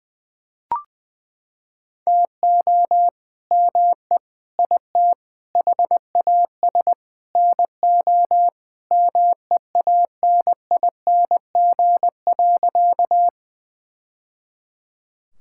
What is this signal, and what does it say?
Morse code, heard as a keyed tone, about 20 wpm, 700 Hz. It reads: TO ME IT HAS NO MEANING.
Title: Sentences from Top 1000 Words - Review Part 6 - 20wpm